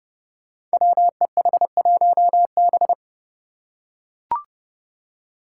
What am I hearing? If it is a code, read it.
Morse code, heard as a keyed tone, about 30 wpm, 700 Hz. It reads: WEH1B
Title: Five Challenging Letter-Number Combinations 30wpm